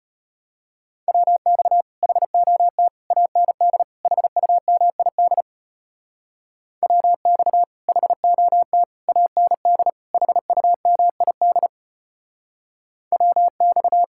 Morse code, heard as a keyed tone, about 33 wpm, 700 Hz. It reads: WX HOT AND HUMID WX HOT AND HUMID WX